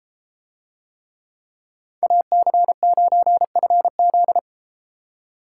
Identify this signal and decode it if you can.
Morse code, heard as a keyed tone, about 31 wpm, 700 Hz. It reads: AC9FZ